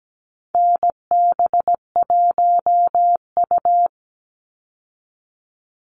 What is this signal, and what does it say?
Morse code, heard as a keyed tone, about 17 wpm, 700 Hz. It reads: NB1U
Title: Call Signs Repeated 3x - 17wpm